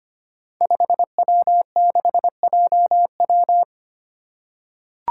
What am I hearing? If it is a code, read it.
Morse code, heard as a keyed tone, about 25 wpm, 700 Hz. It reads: 5W6JW
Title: Five Challenging Letter-Number Combinations 25wpm